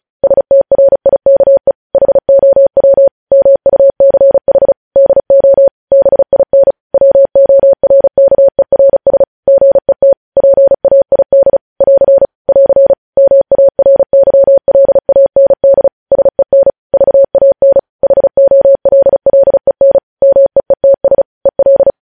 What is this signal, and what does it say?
Morse code, heard as a keyed tone, about 36 wpm, 550 Hz. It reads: STRIKE HOW MUCH DO BIN WORKERS GET PAID <AR> <AR> MARYLAND SEN VAN HOLLEN MEETS EL
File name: podcast / News headlines-1x-35x35